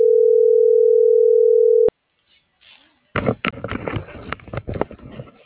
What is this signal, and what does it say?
Ringback tone (North America), 440 + 480 Hz.
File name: CENTRAL SAGOT / out danae.bsagot